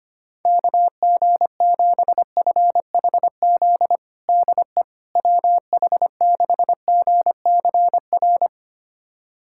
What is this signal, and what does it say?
Morse code, heard as a keyed tone, about 25 wpm, 700 Hz. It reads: KG7FHZ DE WH6GCR
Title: USA Callsign DE Callsign - 6 Characters - 25wpm